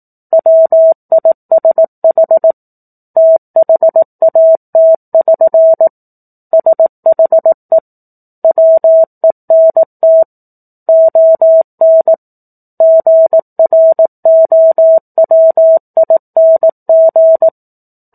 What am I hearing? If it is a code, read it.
Morse code, heard as a keyed tone, about 18 wpm, 650 Hz. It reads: WISH THAT<SN> SHE WENT ON GROWING
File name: 04-Chapter-04 18 WPM